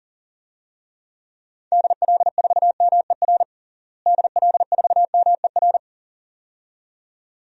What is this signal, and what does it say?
Morse code, heard as a keyed tone, about 40 wpm, 700 Hz. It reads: DL4MER DL4MER
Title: DXCC-100 Call Signs - 6 Characters - Repeated 2x - 40wpm